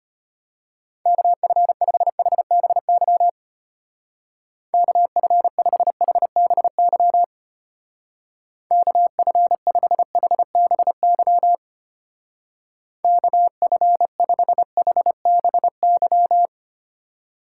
Morse code, heard as a keyed tone, about 31 wpm, 700 Hz. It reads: KF5HBY KF5HBY KF5HBY KF5HBY